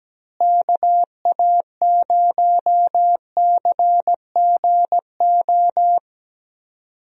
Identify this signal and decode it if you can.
Morse code, heard as a keyed tone, about 17 wpm, 700 Hz. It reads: KA0CGO
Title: USA Call Signs - 6 Characters - Repeated 3x - 17wpm